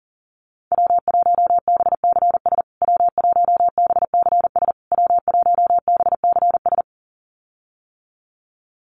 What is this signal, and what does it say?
Morse code, heard as a keyed tone, about 37 wpm, 700 Hz. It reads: W1BCS W1BCS W1BCS
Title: USA Call Signs Repeated 3x 40wpm